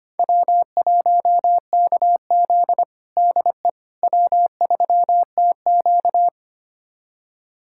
Morse code, heard as a keyed tone, about 25 wpm, 700 Hz. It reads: W1KZ DE W3TQ
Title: USA Callsign DE Callsign - 4 Characters - 25wpm